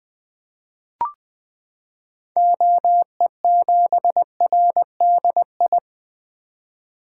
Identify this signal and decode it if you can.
Morse code, heard as a keyed tone, about 20 wpm, 700 Hz. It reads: OE7RDI
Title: 2023 SOTA Activator Call Signs - 20wpm